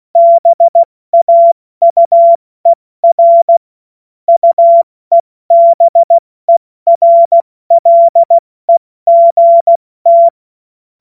Morse code, heard as a keyed tone, about 16 wpm, 680 Hz. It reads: BAUER UEBERLEGT
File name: Witz-16-September-2025-70-F80